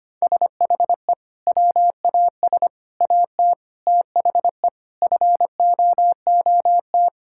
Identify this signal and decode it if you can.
Morse code, heard as a keyed tone, about 25 wpm, 700 Hz. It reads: SHE WAS AT THE FOOT
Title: Sentences from Top 1000 Words - Review Part 5 - 25wpm